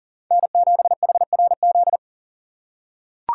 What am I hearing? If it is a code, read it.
Morse code, heard as a keyed tone, about 40 wpm, 700 Hz. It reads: N7HRZ
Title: USA Call Signs - 4x Character Spacing - 40wpm